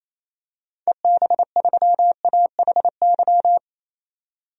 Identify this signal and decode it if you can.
Morse code, heard as a keyed tone, about 28 wpm, 700 Hz. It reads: EB3AHY
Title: DXCC-100 Call Signs - 6 Characters - 28wpm